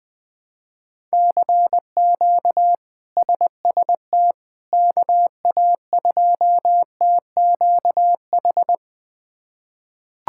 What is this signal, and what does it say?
Morse code, heard as a keyed tone, about 20 wpm, 700 Hz. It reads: CQ SST KA2TQH
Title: K1USN SST QSO Practice - 20wpm